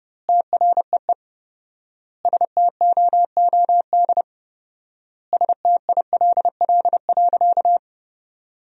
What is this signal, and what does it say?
Morse code, heard as a keyed tone, about 32 wpm, 700 Hz. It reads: TREE STOOD STILL.
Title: 4x Word-Spacing - Sentences from Top 500 Words - 30wpm